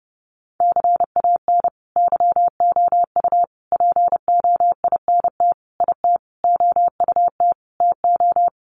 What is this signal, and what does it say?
Morse code, heard as a keyed tone, about 30 wpm, 700 Hz. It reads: CAN YOU POINT IT OUT TO
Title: Sentences from Top 200 Words 30wpm